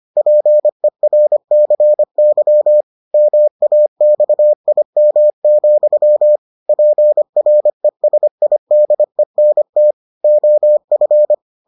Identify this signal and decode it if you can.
Morse code, heard as a keyed tone, about 25 wpm, 600 Hz. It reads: PERCY MAXIM, PRESIDENT OF